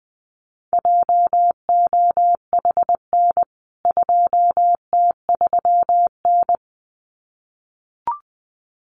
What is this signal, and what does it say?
Morse code, heard as a keyed tone, about 20 wpm, 700 Hz. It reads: JOHN 2T3N
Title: CWT Exchange 20wpm